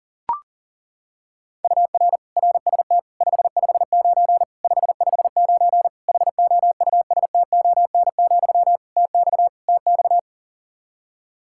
Morse code, heard as a keyed tone, about 37 wpm, 700 Hz. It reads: UR RST 559 559 HOUSTON, TX TX
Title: K1USN SST QSO Practice 40wpm